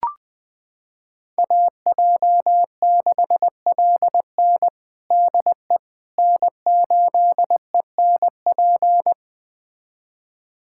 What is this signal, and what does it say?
Morse code, heard as a keyed tone, about 20 wpm, 700 Hz. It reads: AJ6LN DE N8ENP
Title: USA Callsign DE Callsign - 5 Characters - 20wpm